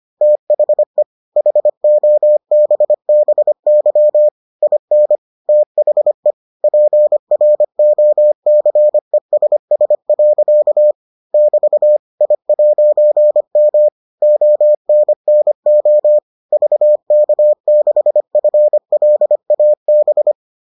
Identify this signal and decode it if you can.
Morse code, heard as a keyed tone, about 25 wpm, 600 Hz. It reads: THE HOBBY IN THE PROCESS. <BT> I'M ONNO VK6FLAB